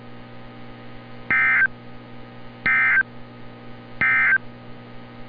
Emergency Alert System (SAME) tones, heard as harsh buzzing bursts. It is heard from 1.2 to 4.5 s.
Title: NOAA Weather Radio - Altoona, PA